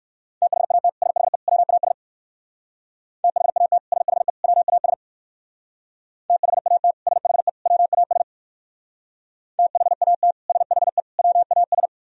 Morse code, heard as a keyed tone, about 62 wpm, 700 Hz. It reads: THAT SHE WAS THAT SHE WAS THAT SHE WAS THAT SHE WAS